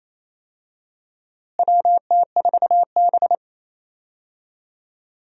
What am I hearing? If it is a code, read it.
Morse code, heard as a keyed tone, about 28 wpm, 700 Hz. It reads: WT4B
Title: USA Call Signs - 4 Characters - Repeated 3x - 28wpm